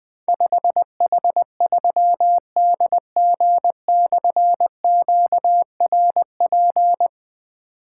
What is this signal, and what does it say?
Morse code, heard as a keyed tone, about 20 wpm, 700 Hz. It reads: 5H3DG/QRP
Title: Difficult Call Signs 20wpm